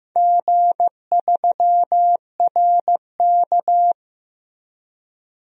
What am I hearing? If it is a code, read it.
Morse code, heard as a keyed tone, about 15 wpm, 700 Hz. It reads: G3RK
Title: DXCC-100 Call Signs - 4 Characters - Repeated 2x - 15wpm